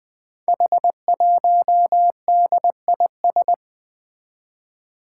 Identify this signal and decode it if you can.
Morse code, heard as a keyed tone, about 21 wpm, 700 Hz. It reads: H1DIS
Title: Five Challenging Letter-Number Combinations 20wpm